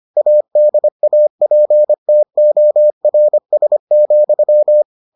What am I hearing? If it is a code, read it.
Morse code, heard as a keyed tone, about 25 wpm, 600 Hz. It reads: ADAPTORS,